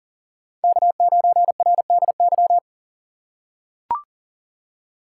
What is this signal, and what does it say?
Morse code, heard as a keyed tone, about 40 wpm, 700 Hz. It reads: K9RDY
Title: USA Call Signs - 2x Character Spacing - 40wpm